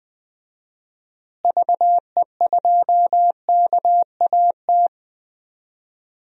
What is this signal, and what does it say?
Morse code, heard as a keyed tone, about 20 wpm, 700 Hz. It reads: VE2KAT
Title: DXCC-100 Call Signs - 6 Characters - Repeated 2x - 20wpm